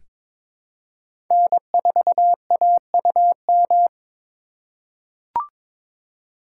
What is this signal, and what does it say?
Morse code, heard as a keyed tone, about 22 wpm, 700 Hz. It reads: N4AUM
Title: USA Call Signs - 5 Characters - 2x Character Spacing - 22wpm